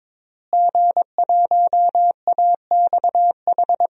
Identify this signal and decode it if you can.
Morse code, heard as a keyed tone, about 22 wpm, 700 Hz. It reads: G1AXH
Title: DXCC-100 Call Signs - 5 Characters - Repeated 2x - 22wpm